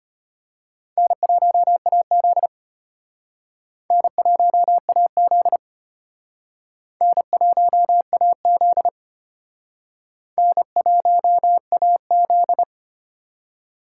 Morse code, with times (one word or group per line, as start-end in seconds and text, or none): 1.0-2.5 N1AZ
3.9-5.6 N1AZ
7.0-8.9 N1AZ
10.4-12.6 N1AZ